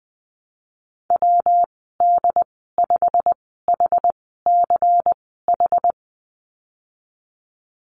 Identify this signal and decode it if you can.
Morse code, heard as a keyed tone, about 19 wpm, spaced as for 14 wpm, 700 Hz. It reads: WD5HCH
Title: 186 Lesson 30g - Letter b - Call Signs 20wpm